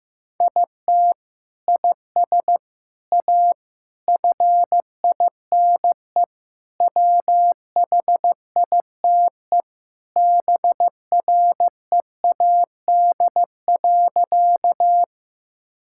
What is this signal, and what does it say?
Morse code, heard as a keyed tone, about 15 wpm, 700 Hz. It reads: IT IS A FINE WHITE BREAD.
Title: Sentences from Top 1600 Words 15wpm